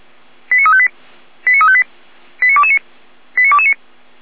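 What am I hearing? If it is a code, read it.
Five-tone selective call (ZVEI), heard as quick runs of five tones, tones of about 70 ms. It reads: 89358 89358 89229 89229
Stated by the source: ZVEI five-tone selective call received by SDR.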